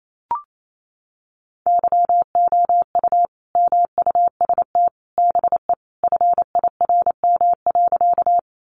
Morse code, heard as a keyed tone, about 28 wpm, 700 Hz. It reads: YOU MUST BE FIRM.
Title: Sentences from Top 1400 Words 28wpm